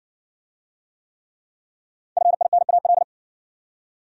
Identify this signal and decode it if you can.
Morse code, heard as a keyed tone, about 58 wpm, 700 Hz. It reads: FINAL